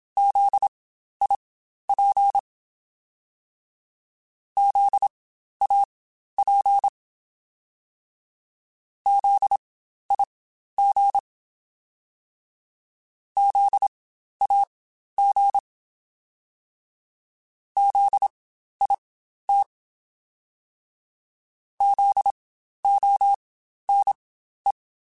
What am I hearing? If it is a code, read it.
Morse code, heard as a keyed tone, about 27 wpm, spaced as for 12 wpm, 800 Hz. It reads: ZIP ZAP ZIG ZAG ZIT ZONE
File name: Spaced Alphas / Spaced Alphas mp3s / Zs sp @25